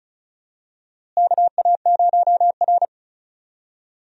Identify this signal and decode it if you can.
Morse code, heard as a keyed tone, about 34 wpm, 700 Hz. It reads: KA0R
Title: USA Call Signs - 4 Characters - Repeated 3x - 35wpm